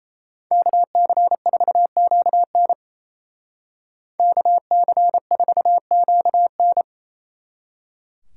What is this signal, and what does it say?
Morse code, heard as a keyed tone, about 33 wpm, 700 Hz. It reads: KC4QN KC4QN
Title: Speed Racing - USA Call Signs 28wpm